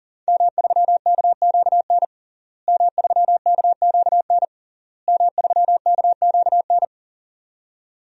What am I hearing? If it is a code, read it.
Morse code, heard as a keyed tone, about 37 wpm, 700 Hz. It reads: M3KQN M3KQN M3KQN
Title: DXCC-100 Call Signs - 5 Characters - Repeated 3x - 40wpm